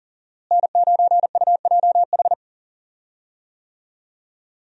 Morse code, heard as a keyed tone, about 40 wpm, 700 Hz. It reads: N9UJH
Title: USA Call Signs - 5 Characters - Repeated 3x - 40wpm